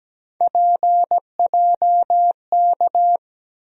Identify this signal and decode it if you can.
Morse code, heard as a keyed tone, about 17 wpm, 700 Hz. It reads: PJK
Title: USA Call Sign Postfixes - 3 Letters 17wpm